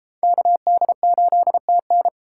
Morse code, heard as a keyed tone, about 30 wpm, 700 Hz. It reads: KD8TN